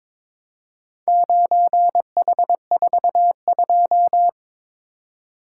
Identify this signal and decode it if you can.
Morse code, heard as a keyed tone, about 22 wpm, 700 Hz. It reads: 9H42
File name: practice / 22wpm / Four Letter-Number Combinations - 22wpm